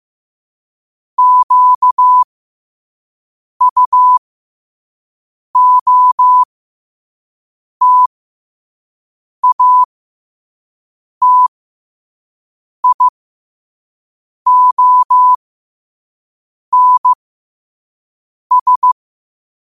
Morse code, heard as a keyed tone, about 15 wpm, spaced as for 5 wpm, 1000 Hz. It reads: QUOTATIONS
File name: qotdfiles / 5wpm